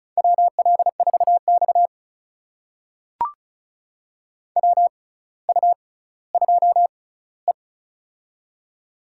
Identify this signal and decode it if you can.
Morse code, heard as a keyed tone, about 36 wpm, spaced as for 18 wpm, 700 Hz. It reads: WL4X WU2E
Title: USA Call Signs - 4 Characters - 6x Character Spacing - 35wpm